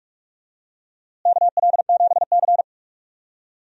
Morse code, heard as a keyed tone, about 46 wpm, 700 Hz. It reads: KL7C
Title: DXCC-100 Call Signs - 4 Characters - Repeated 2x - 45wpm